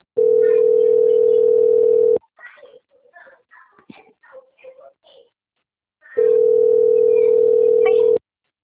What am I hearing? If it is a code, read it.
Ringback tone (North America), 440 + 480 Hz.
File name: out- - - - - .30998